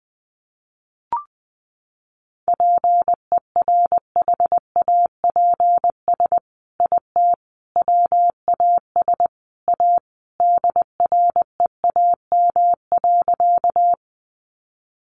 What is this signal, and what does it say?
Morse code, heard as a keyed tone, about 20 wpm, 700 Hz. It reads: PERHAPS IT WAS A DREAM.
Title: Sentences from Top 1500 Words 20wpm